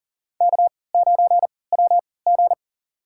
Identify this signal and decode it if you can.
Morse code, heard as a keyed tone, about 40 wpm, spaced as for 23 wpm, 700 Hz. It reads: K9WG
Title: USA Call Signs - 4 Characters - 3x Character Spacing - 40wpm